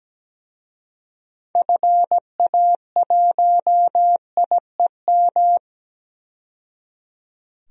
Morse code, heard as a keyed tone, about 17 wpm, 700 Hz. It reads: FA1IEM
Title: DXCC-100 Call Signs - 6 Characters - 17wpm